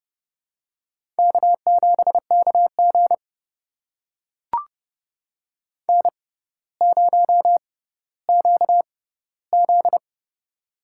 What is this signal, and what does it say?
Morse code, heard as a keyed tone, about 30 wpm, spaced as for 15 wpm, 700 Hz. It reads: K7KG N0QZ
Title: USA Call Signs - 4 Characters - 6x Character Spacing - 30wpm